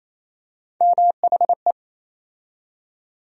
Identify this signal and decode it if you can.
Morse code, heard as a keyed tone, about 28 wpm, 700 Hz. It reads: MHE
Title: USA Call Sign Postfixes - 3 Letters 28wpm